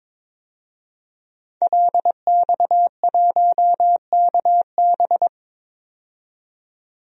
Morse code, heard as a keyed tone, about 21 wpm, 700 Hz. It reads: LX1KB
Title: DXCC-100 Call Signs - 5 Characters - 22wpm